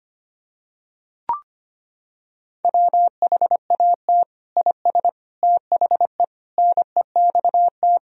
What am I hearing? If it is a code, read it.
Morse code, heard as a keyed tone, about 25 wpm, 700 Hz. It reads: WHAT IS THE NEXT
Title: Sentences from Top 1000 Words - Review Part 1 - 25wpm